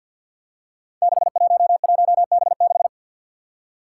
Morse code, heard as a keyed tone, about 48 wpm, 700 Hz. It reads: 611B6
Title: Five Challenging Letter-Number Combinations 50wpm